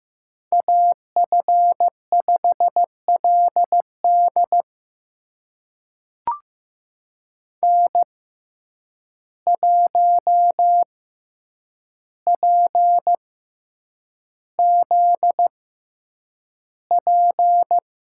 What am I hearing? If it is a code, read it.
Morse code, heard as a keyed tone, about 15 wpm, spaced as for 8 wpm, 700 Hz. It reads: AF5LD N1PZP
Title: USA Call Signs - 5 Characters - 6x Character Spacing - 15wpm